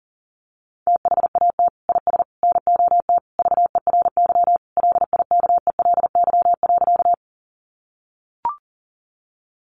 Morse code, heard as a keyed tone, about 40 wpm, 700 Hz. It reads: THAT IS NOT VERY LIKELY.
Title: Sentences from Top 600 Words 40wpm